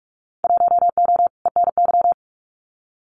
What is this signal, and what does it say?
Morse code, heard as a keyed tone, about 46 wpm, 700 Hz. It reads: 1O ENY
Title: Winter Field Day Exchanges 45wpm